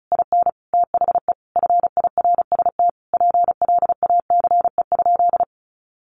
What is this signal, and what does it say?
Morse code, heard as a keyed tone, about 34 wpm, 700 Hz. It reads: IN THE FIRST PLACE?
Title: Sentences from Top 200 Words 35wpm